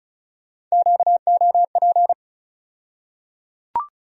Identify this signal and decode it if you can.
Morse code, heard as a keyed tone, about 36 wpm, 700 Hz. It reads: QOP
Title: USA Call Sign Postfixes - 3 Letters 35wpm